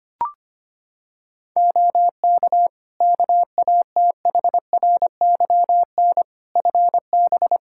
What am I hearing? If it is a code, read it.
Morse code, heard as a keyed tone, about 25 wpm, 700 Hz. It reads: OK KATHRYN FB
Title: Practice QSO - Exchange 2 - Info - 25wpm